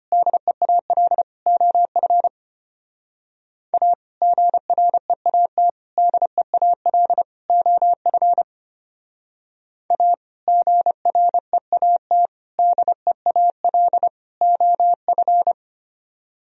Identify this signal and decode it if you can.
Morse code, heard as a keyed tone, about 29 wpm, 700 Hz. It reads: DEAL OF A GREAT DEAL OF A GREAT DEAL OF
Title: Speed-Racing - Sets of 4 Words 25wpm